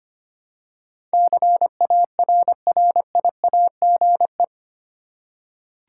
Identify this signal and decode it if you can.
Morse code, heard as a keyed tone, about 25 wpm, 700 Hz. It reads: CARRIAGE
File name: Eight Letter Words - 25wpm